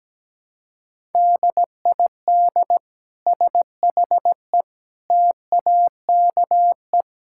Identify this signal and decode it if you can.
Morse code, heard as a keyed tone, about 17 wpm, 700 Hz. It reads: DID SHE TAKE
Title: Sentences from Top 1000 Words - Review Part 6 - 17wpm